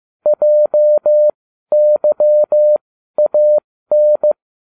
Morse code, heard as a keyed tone, about 15 wpm, spaced as for 12 wpm, 600 Hz. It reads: JYAN